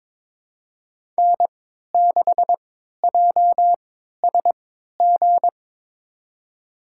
Morse code, heard as a keyed tone, about 21 wpm, spaced as for 12 wpm, 700 Hz. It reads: N6JSG
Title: USA Call Signs - 5 Characters - 3x Character Spacing - 22wpm